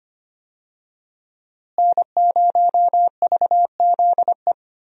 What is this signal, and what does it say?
Morse code, heard as a keyed tone, about 25 wpm, 700 Hz. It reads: N0VZE
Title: USA Call Signs - 5 Characters - Repeated 3x - 25wpm